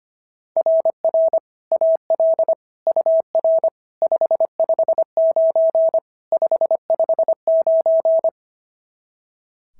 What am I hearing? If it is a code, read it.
Morse code, heard as a keyed tone, about 22 wpm, 650 Hz. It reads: RR AL UR 559 559
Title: SOTA QSO Practice 25wpm